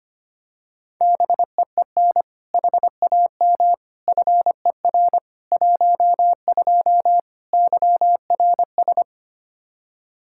Morse code, heard as a keyed tone, about 25 wpm, 700 Hz. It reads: BEEN HAM FER 12 YRS
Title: Practice QSO - Exchange 3 - More Info - 25wpm